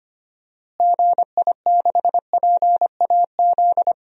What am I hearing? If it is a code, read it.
Morse code, heard as a keyed tone, about 25 wpm, 700 Hz. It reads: GI6PAZ
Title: DXCC-100 Call Signs - 6 Characters - Repeated 3x - 25wpm